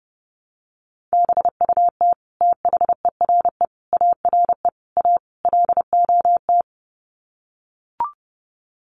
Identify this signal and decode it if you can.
Morse code, heard as a keyed tone, about 30 wpm, 700 Hz. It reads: BUT THERE ARE A LOT
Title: Sets of 5 Words Encore2 30wpm